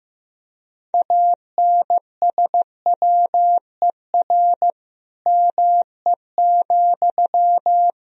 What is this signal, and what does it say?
Morse code, heard as a keyed tone, about 15 wpm, 700 Hz. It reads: ANSWER ME,
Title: Sentences from Top 1000 Words - Review Part 4 - 15wpm